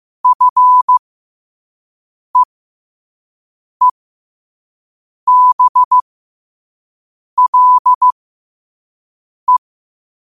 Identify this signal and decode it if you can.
Morse code, heard as a keyed tone, about 15 wpm, spaced as for 5 wpm, 1000 Hz. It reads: FEEBLE
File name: qotdfiles / 5wpm